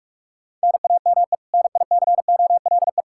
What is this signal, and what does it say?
Morse code, heard as a keyed tone, about 46 wpm, 700 Hz. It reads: NAME NICOLE
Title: Practice QSO - Exchange 1 - Beginning - 45wpm